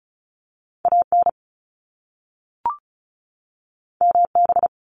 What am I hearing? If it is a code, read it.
Morse code, heard as a keyed tone, about 36 wpm, 700 Hz. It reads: AN MB
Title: Top Two Letter Combinations in Words 35wpm